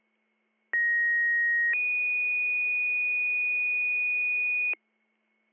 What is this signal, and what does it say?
Two-tone page, heard as a one-second tone, then a three-second tone, about 1900 then 2450 Hz, with noise. It is heard from 0.7 to 4.7 s.